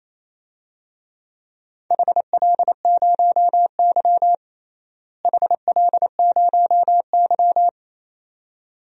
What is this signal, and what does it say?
Morse code, heard as a keyed tone, about 28 wpm, 700 Hz. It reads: HL0Y HL0Y
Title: DXCC-100 Call Signs - 4 Characters - Repeated 2x - 28wpm